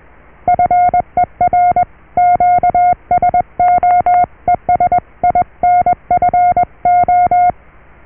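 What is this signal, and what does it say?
Morse code, heard as a keyed tone, about 21 wpm, 710 Hz. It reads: FER QSO ES INFO
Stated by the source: Morse QSO on 14 MHz with a German station, received at RST 579.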